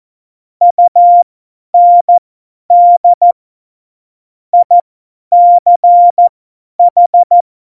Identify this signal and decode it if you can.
Morse code, heard as a keyed tone, about 14 wpm, spaced as for 10 wpm, 700 Hz. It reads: UND ICH
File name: Morsekurs / Witze / Witz 08. September 2020, 50 F70